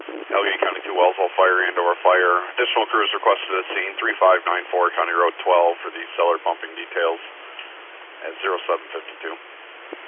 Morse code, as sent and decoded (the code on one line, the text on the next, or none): none